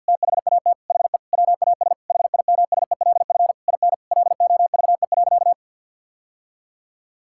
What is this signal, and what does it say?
Morse code, heard as a keyed tone, about 50 wpm, 700 Hz. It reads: THAT HE WAS HIMSELF IN LOVE.
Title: Sentences from Top 1600 Words 50wpm